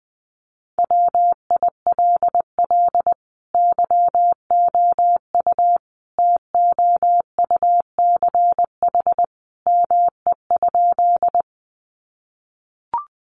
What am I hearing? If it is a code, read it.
Morse code, heard as a keyed tone, about 20 wpm, 700 Hz. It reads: WILL YOU TOUCH ME?
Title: Sentences from Top 1200 Words 20wpm